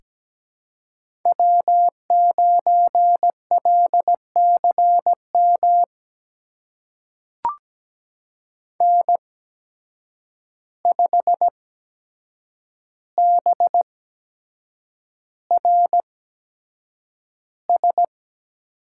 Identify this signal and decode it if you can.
Morse code, heard as a keyed tone, about 17 wpm, spaced as for 7 wpm, 700 Hz. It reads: W9LCM N5BRS
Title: USA Call Signs - 5 Characters - 8x Character Spacing - 17wpm